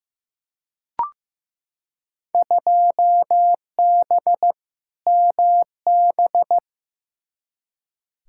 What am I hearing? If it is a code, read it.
Morse code, heard as a keyed tone, about 16 wpm, 700 Hz. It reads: 2B MB
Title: Field Day Exchanges Encore - 15wpm